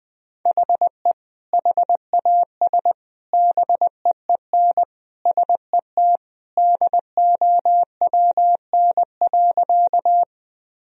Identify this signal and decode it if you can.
Morse code, heard as a keyed tone, about 21 wpm, 700 Hz. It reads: HE HAS BEEN SET DOWN.
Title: Sentences from Top 1000 Words - Review Part 1 - 20wpm